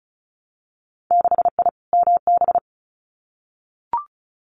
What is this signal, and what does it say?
Morse code, heard as a keyed tone, about 36 wpm, 700 Hz. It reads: 6I MB
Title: Winter Field Day Exchanges Encore 35wpm